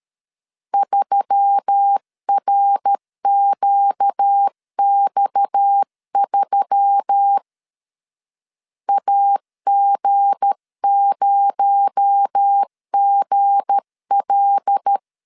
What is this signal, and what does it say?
Morse code, heard as a keyed tone, about 13 wpm, 800 Hz. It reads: 3RQX3 AG0GL
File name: LEZIONI / 7 GF9Z / 7 G 12wpm 2055